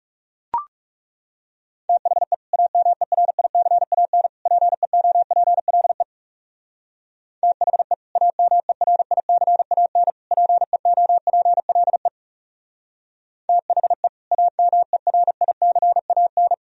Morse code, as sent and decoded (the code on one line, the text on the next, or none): - .... / .- -- . .-. .. -.-. .- -. / .--. --- .--. .-.. / - .... / .- -- . .-. .. -.-. .- -. / .--. --- .--. .-.. / - .... / .- -- . .-. .. -.-. .- -.
THE AMERICAN PEOPLE THE AMERICAN PEOPLE THE AMERICAN